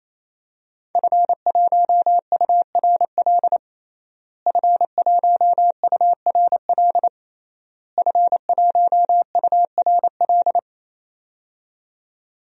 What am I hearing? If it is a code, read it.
Morse code, heard as a keyed tone, about 28 wpm, 700 Hz. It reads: F1URL F1URL F1URL
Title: DXCC-100 Call Signs - 5 Characters - Repeated 3x - 28wpm